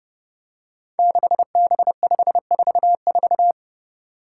Morse code, heard as a keyed tone, about 30 wpm, 700 Hz. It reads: 6B544
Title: Five Challenging Letter-Number Combinations 30wpm